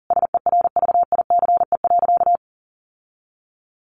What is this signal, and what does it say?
Morse code, heard as a keyed tone, about 40 wpm, 700 Hz. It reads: SERVICE.
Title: Sentences from Top 1400 Words 40wpm